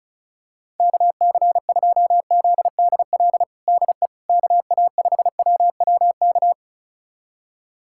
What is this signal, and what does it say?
Morse code, heard as a keyed tone, about 36 wpm, 700 Hz. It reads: KC2ZDL DE KA5WWK
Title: USA Callsign DE Callsign 35wpm